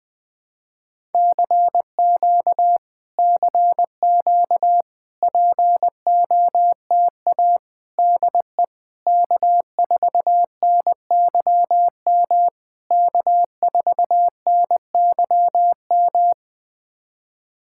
Morse code, heard as a keyed tone, about 20 wpm, 700 Hz. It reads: CQ CQ POTA DE K4NYM K4NYM
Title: POTA QSO Practice - 20wpm